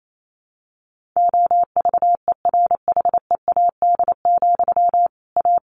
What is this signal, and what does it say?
Morse code, heard as a keyed tone, about 28 wpm, 700 Hz. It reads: OVERHEAD, A
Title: A Sound Of Thunder 28wpm